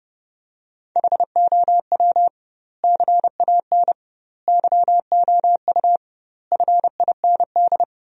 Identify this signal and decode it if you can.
Morse code, heard as a keyed tone, about 30 wpm, 700 Hz. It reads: HOW CAN YOU FIND
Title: 2x Word-Spacing - Sentences from Top 200 Words - 30wpm